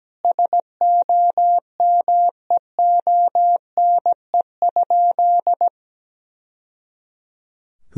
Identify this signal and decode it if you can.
Morse code, heard as a keyed tone, about 17 wpm, 700 Hz. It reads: SOMEONE?
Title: Sentences from Top 1000 Words - Review Part 4 - 17wpm